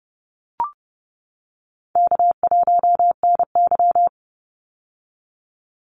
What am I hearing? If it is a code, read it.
Morse code, heard as a keyed tone, about 30 wpm, 700 Hz. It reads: K1NY
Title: Easy Call Signs - 30wpm